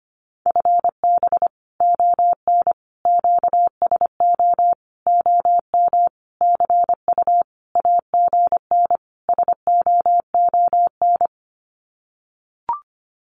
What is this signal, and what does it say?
Morse code, heard as a keyed tone, about 25 wpm, 700 Hz. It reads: FB ON QSO OM CU AGN SOON
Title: Common QSO Phrases 25wpm